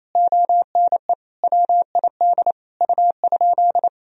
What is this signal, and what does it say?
Morse code, heard as a keyed tone, about 28 wpm, 700 Hz. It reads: ONE WID U?
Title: Sentences from Top 500 Words and Abbreviations 28wpm